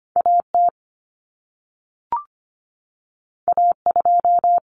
Morse code, heard as a keyed tone, about 25 wpm, 700 Hz. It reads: AT A2
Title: Common Cut Numbers - 2-Digit 25wpm